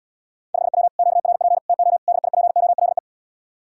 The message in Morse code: 5W QRP TO DIPOLE

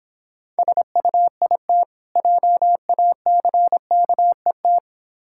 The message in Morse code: SUIT JACKET

